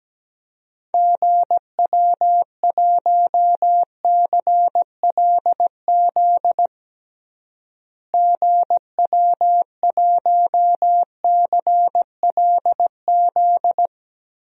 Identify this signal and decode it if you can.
Morse code, heard as a keyed tone, about 17 wpm, 700 Hz. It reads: GW1CLZ GW1CLZ
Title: DXCC-100 Call Signs - 6 Characters - Repeated 3x - 17wpm